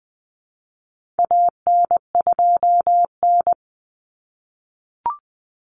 Morse code, heard as a keyed tone, about 20 wpm, 700 Hz. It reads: AN2N
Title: Common Cut Numbers - 4-Digit 20wpm